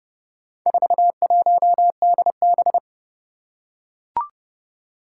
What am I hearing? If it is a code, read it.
Morse code, heard as a keyed tone, about 30 wpm, 700 Hz. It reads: 41DB
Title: Four Challenging Letter-Number Combinations 30wpm